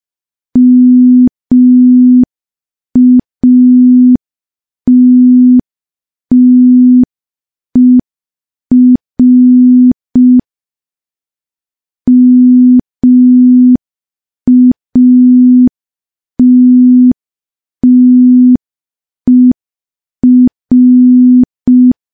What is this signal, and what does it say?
Morse code, heard as a keyed tone, about 5 wpm, 250 Hz. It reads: MATTER MATTER
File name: morse / morse-training-05WPM